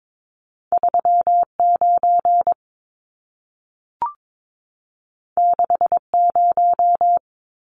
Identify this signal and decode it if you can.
Morse code, heard as a keyed tone, about 22 wpm, 700 Hz. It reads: 39 60
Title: Numbers - 2-Digit 22wpm